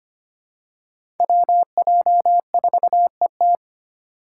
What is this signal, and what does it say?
Morse code, heard as a keyed tone, about 25 wpm, 700 Hz. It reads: WJ4ET
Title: USA Call Signs - 3x Character Spacing - 25wpm